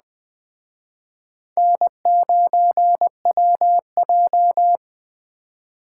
Morse code, heard as a keyed tone, about 20 wpm, 700 Hz. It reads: N9WJ
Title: Speed Racing - USA 1x2 Call Signs 17wpm